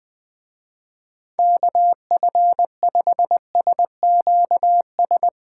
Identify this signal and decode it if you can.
Morse code, heard as a keyed tone, about 21 wpm, 700 Hz. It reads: KF5SQS